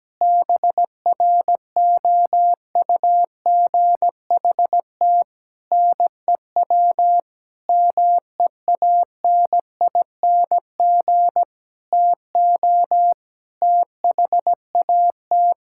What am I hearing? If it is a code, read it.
Morse code, heard as a keyed tone, about 17 wpm, 700 Hz. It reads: BROUGHT NEW MEANING TO THAT
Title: Sentences from Top 1600 Words 17wpm